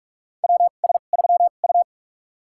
Morse code, heard as a keyed tone, about 44 wpm, spaced as for 32 wpm, 700 Hz. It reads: WS3U
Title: USA Call Signs - 4 Characters - 2x Character Spacing - 45wpm